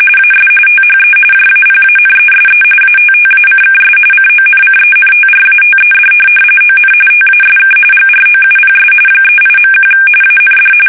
RTTY, about 45.45 baud. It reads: AS THE JAPANESE ATTACKS CONCENTRATED ON TASK GROUPS 38.1 AND
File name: fsk850 / USS-Massachusetts